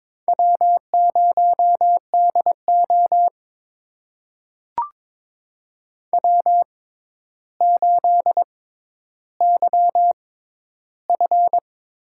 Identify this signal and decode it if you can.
Morse code, heard as a keyed tone, about 23 wpm, spaced as for 11 wpm, 700 Hz. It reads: W0DO W8YF
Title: USA Call Signs - 4 Characters - 6x Character Spacing - 22wpm